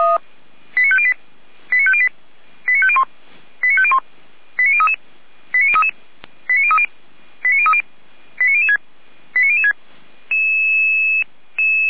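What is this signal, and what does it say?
Five-tone selective call (ZVEI), heard as quick runs of five tones, tones of about 70 ms. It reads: 89508 89508 89501 89501 89033 89033 89030 89030 89006 89006
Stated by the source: Five-tone ZVEI selective call 89099 received by SDR, feuerwehr-orlamuende.org fire brigade audio server.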